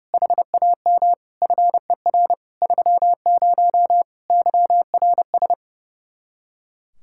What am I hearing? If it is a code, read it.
Morse code, heard as a keyed tone, about 37 wpm, 700 Hz. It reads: HAM FER 30 YRS
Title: Speed-Racing Common QSO Phrases - 30wpm